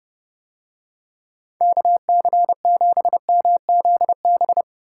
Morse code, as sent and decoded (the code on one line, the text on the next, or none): -.- -.-. --... -- --.. -...
KC7MZB